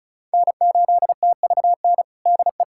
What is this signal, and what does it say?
Morse code, heard as a keyed tone, about 36 wpm, 700 Hz. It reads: N8TVN DE
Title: USA Callsign DE Callsign 35wpm